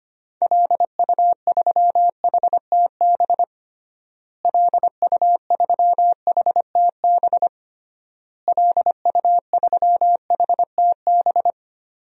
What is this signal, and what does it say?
Morse code, heard as a keyed tone, about 25 wpm, 700 Hz. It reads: LU3HTB LU3HTB LU3HTB
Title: DXCC-100 Call Signs - 6 Characters - Repeated 3x - 25wpm